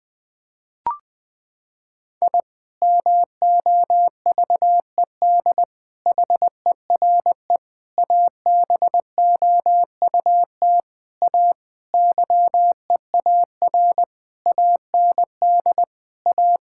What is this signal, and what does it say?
Morse code, heard as a keyed tone, about 20 wpm, 700 Hz. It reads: I MOVED HERE ABOUT A YEAR AND A
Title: Sentences with 7-Word Phrases - 20wpm